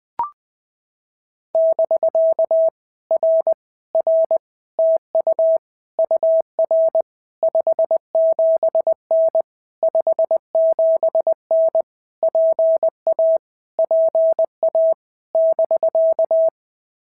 Morse code, heard as a keyed tone, about 20 wpm, 650 Hz. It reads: <BK> R R TU UR 57N 57N PA PA <BK>
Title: POTA QSO Practice - 20wpm